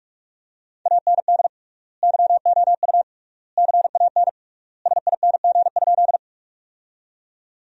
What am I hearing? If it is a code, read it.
Morse code, heard as a keyed tone, about 46 wpm, 700 Hz. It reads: AND YOU CAN SING?
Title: 3x Word-Spacing - Sentences from Top 400 Words - 45wpm